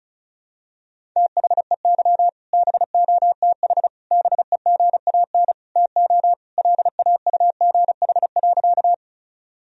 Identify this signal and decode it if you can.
Morse code, heard as a keyed tone, about 36 wpm, 700 Hz. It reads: THEY BOTH BEGAN TO LAUGH.
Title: Sentences from Top 1000 Words - Review Part 1 - 35wpm